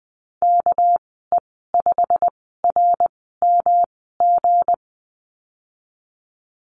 Morse code, heard as a keyed tone, about 20 wpm, spaced as for 15 wpm, 700 Hz. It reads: KE5RMG
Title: 200 Lesson 32g - Letter k - Call Signs 20wpm